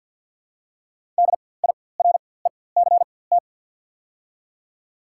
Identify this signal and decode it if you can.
Morse code, heard as a keyed tone, about 54 wpm, spaced as for 24 wpm, 690 Hz. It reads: DIRECT
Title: 4x Character Spacing - Top 500 Words - 50wpm